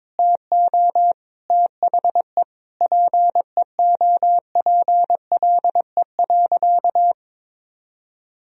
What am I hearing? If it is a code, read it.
Morse code, heard as a keyed tone, about 22 wpm, 700 Hz. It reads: TO THE PEOPLE.
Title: Sentences from Top 400 Words and Abbreviations 22wpm